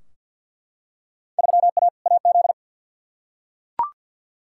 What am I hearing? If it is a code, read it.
Morse code, heard as a keyed tone, about 49 wpm, 700 Hz. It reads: 3A AZ
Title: Field Day Exchanges - 50wpm